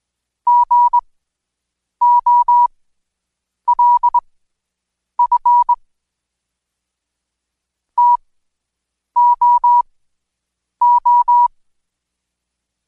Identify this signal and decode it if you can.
Morse code, heard as a keyed tone, about 20 wpm, spaced as for 7 wpm, 970 Hz. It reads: GOLF TOO
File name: Code Quick / AudioDownld2 / Seg112-Day17-DayDrill 17